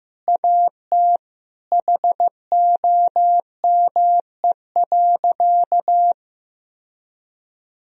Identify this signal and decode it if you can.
Morse code, heard as a keyed tone, about 16 wpm, 700 Hz. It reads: AT HOME.
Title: Sentences from Top 1000 Words - Review Part 3 - 15wpm